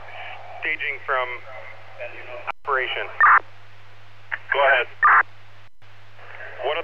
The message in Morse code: none

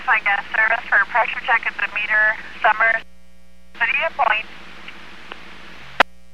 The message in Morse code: none

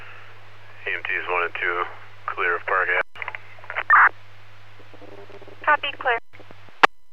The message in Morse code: none

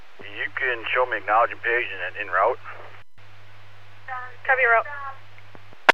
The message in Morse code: none

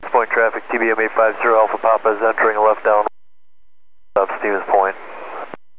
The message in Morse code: none

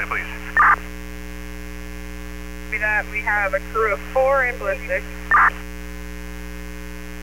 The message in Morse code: none

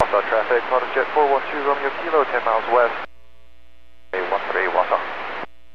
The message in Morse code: none

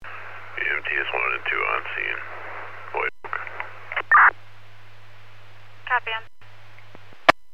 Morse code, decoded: none

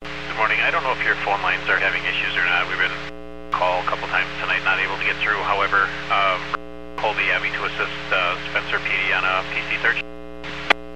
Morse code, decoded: none

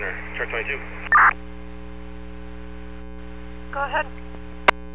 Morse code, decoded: none

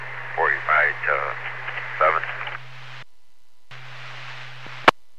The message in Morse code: none

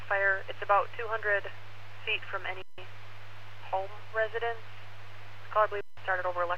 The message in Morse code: none

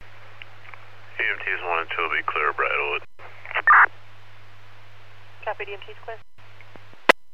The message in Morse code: none

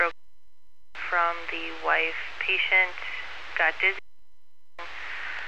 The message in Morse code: none

